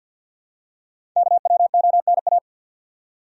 Morse code, heard as a keyed tone, about 47 wpm, 700 Hz. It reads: KWONA
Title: Code Groups - 5 Letters - 50wpm